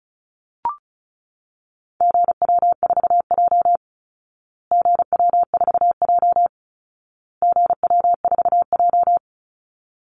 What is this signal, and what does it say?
Morse code, heard as a keyed tone, about 34 wpm, 700 Hz. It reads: GW4J GW4J GW4J
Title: Call Signs Repeated 2x - 35wpm